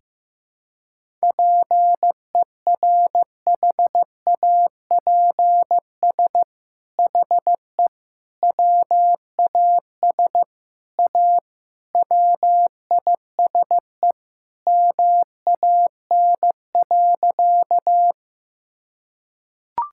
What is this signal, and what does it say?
Morse code, heard as a keyed tone, about 15 wpm, 700 Hz. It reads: PERHAPS HE WAS A WISE MAN.
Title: Sentences from Top 1600 Words 15wpm